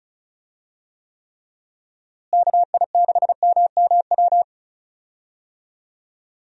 Morse code, heard as a keyed tone, about 34 wpm, 700 Hz. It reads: KI6MMW
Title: USA Call Signs - 6 Characters - Repeated 2x - 35wpm